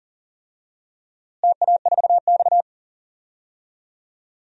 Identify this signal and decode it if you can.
Morse code, heard as a keyed tone, about 38 wpm, 700 Hz. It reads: TA4X
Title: DXCC-100 Call Signs - 4 Characters - Repeated 3x - 40wpm